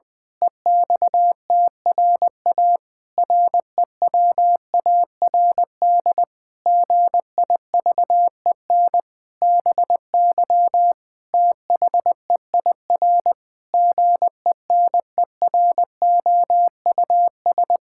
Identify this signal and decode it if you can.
Morse code, heard as a keyed tone, about 20 wpm, 700 Hz. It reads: EXTRA REWARD GIVEN BY THEIR GENEROUS
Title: Pangram Challenge v2 - 20wpm